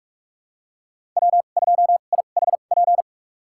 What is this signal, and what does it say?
Morse code, heard as a keyed tone, about 46 wpm, spaced as for 32 wpm, 700 Hz. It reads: W2IHP
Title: USA Call Signs - 5 Characters - 2x Character Spacing - 45wpm